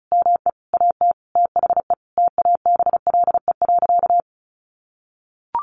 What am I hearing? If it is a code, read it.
Morse code, heard as a keyed tone, about 34 wpm, 700 Hz. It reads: ME AT THE TABLE.
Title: Sentences from Top 1100 Words 35wpm